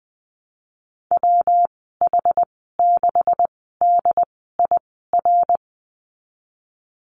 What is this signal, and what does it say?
Morse code, heard as a keyed tone, about 20 wpm, spaced as for 15 wpm, 700 Hz. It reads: WH6DIR
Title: 161 Lesson 26g - Letter g - Call Signs 20wpm